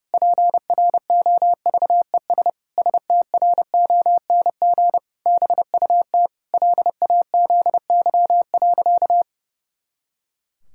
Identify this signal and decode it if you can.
Morse code, heard as a keyed tone, about 30 wpm, 700 Hz. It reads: PROVES STRONG BUT LAZY.
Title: Pangram Challenge v2 - 30wpm